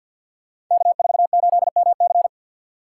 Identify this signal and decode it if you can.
Morse code, heard as a keyed tone, about 52 wpm, 700 Hz. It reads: K48MC
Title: Code Groups - 5 Letter:Numbers - 50wpm